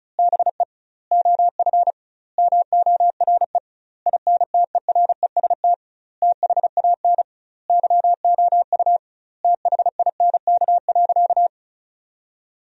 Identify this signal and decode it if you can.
Morse code, heard as a keyed tone, about 34 wpm, 700 Hz. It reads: BE OF MORE INTEREST THAN YOU THINK.
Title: 2x Word-Spacing - Sentences from Top 400 Words - 35wpm